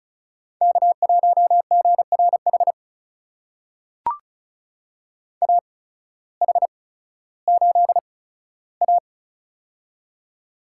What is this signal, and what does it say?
Morse code, heard as a keyed tone, about 36 wpm, spaced as for 16 wpm, 700 Hz. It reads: K1GRH AH8A
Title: USA Call Signs - 8x Character Spacing - 35wpm